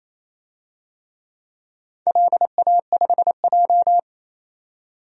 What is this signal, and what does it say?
Morse code, heard as a keyed tone, about 28 wpm, 700 Hz. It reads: LA5J
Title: DXCC-100 Call Signs - 4 Characters - 28wpm